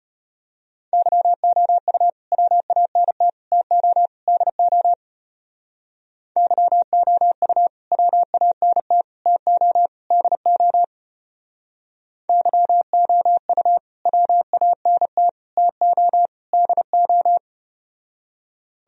Morse code, heard as a keyed tone, about 33 wpm, 700 Hz. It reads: YOU WANT TO DO YOU WANT TO DO YOU WANT TO DO